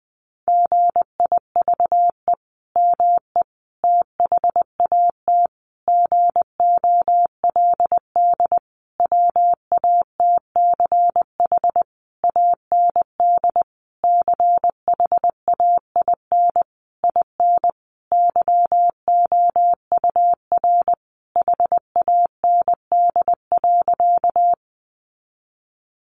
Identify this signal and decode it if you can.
Morse code, heard as a keyed tone, about 20 wpm, 700 Hz. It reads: GIVE ME THAT GOLD WATCH AND CHAIN IN YOUR HAND.
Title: Sentences from Top 1500 Words 20wpm